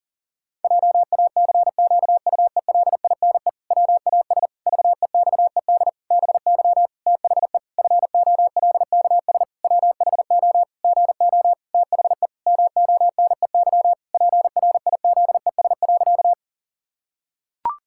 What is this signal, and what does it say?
Morse code, heard as a keyed tone, about 40 wpm, 700 Hz. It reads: JACQUELINE WAS VEXED BY THE FOLKS WHO GO THE MONEY PRIZES.
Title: Pangram Challenge v2 - 40wpm